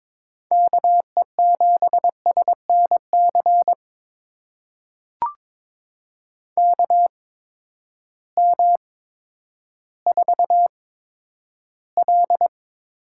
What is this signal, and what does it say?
Morse code, heard as a keyed tone, about 22 wpm, spaced as for 11 wpm, 700 Hz. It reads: KE7SNC KM4L
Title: USA Call Signs - 6 Characters - 8x Character Spacing - 22wpm